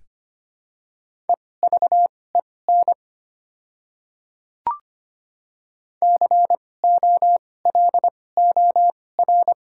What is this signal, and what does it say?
Morse code, heard as a keyed tone, about 25 wpm, spaced as for 18 wpm, 690 Hz. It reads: EVEN COLOR
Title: 2x Character Spacing - Top 300 Words - 25wpm